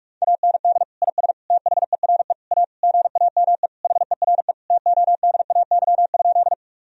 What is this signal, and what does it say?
Morse code, heard as a keyed tone, about 41 wpm, 700 Hz. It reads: AND IS THERE A GAME HERE TODAY?
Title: Sentences from Top 1000 Words - Review Part 3 - 45wpm